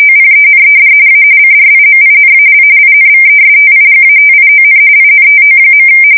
RTTY, about 45.45 baud. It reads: OPENED A CRIMINAL INVESTIGATION INTO.
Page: 170 Shift 2125/2295